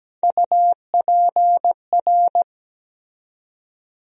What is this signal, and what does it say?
Morse code, heard as a keyed tone, about 17 wpm, 700 Hz. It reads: UPR
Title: USA Call Sign Postfixes - 3 Letters 17wpm